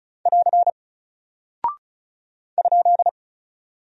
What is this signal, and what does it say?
Morse code, heard as a keyed tone, about 36 wpm, 700 Hz. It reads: <AR> ?
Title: Single Character (Letter:Number:Punctuation:Prosign) Alt 35wpm